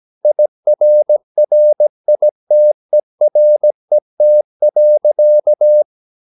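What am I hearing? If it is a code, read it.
Morse code, heard as a keyed tone, about 17 wpm, 600 Hz. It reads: IRRITERET.
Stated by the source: Slow (QRS) Morse practice at 17 wpm, from CW med Gnister.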